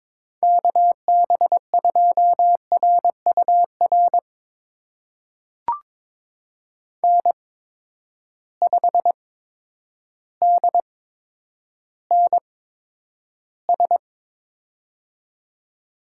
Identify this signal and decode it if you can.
Morse code, heard as a keyed tone, about 21 wpm, spaced as for 10 wpm, 700 Hz. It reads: KB2RUR N5DNS
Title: USA Call Signs - 8x Character Spacing - 22wpm